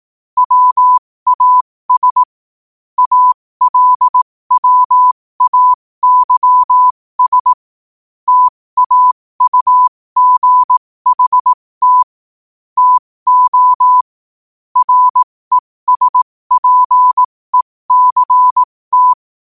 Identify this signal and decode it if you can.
Morse code, heard as a keyed tone, about 18 wpm, spaced as for 15 wpm, 1000 Hz. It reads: WAS ALWAYS TAUGHT TO RESPECT
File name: qotdfiles / 15wpm